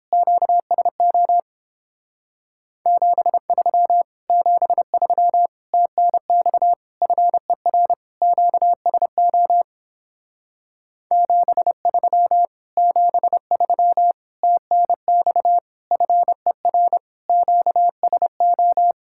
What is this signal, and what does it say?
Morse code, heard as a keyed tone, about 29 wpm, 700 Hz. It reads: QSO 73 73 TNX FER QSO 73 73 TNX FER QSO